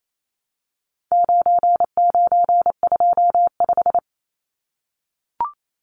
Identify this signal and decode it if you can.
Morse code, heard as a keyed tone, about 28 wpm, 700 Hz. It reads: 9925
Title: Numbers - 4-Digit 28wpm